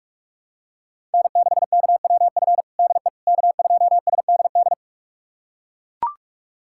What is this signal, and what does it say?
Morse code, heard as a keyed tone, about 42 wpm, 700 Hz. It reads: N6KWF DE K2SDD
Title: USA Callsign DE Callsign - 5 Characters - 45wpm